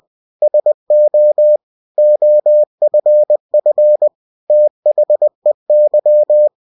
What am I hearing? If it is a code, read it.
Morse code, heard as a keyed tone, about 20 wpm, 600 Hz. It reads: SO OFF THEY
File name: cw-books / winnie-the-pooh / winnie-20wpm-page-0043